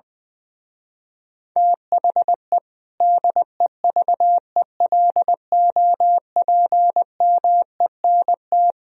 Morse code, heard as a keyed tone, about 21 wpm, 700 Hz. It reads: THE DEVELOPMENT
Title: Speed-Racing - Sets of 3 Words 20wpm